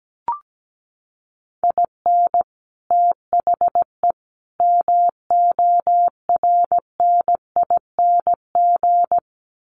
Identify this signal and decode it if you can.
Morse code, heard as a keyed tone, about 17 wpm, 700 Hz. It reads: IN THE MORNING
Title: Sets of 3 Words 17wpm